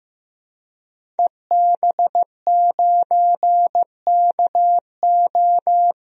Morse code, heard as a keyed tone, about 16 wpm, 700 Hz. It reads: EB9KO
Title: DXCC-100 Call Signs - 5 Characters - Repeated 3x - 15wpm